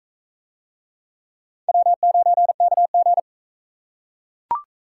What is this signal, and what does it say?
Morse code, heard as a keyed tone, about 41 wpm, 700 Hz. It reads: W9KG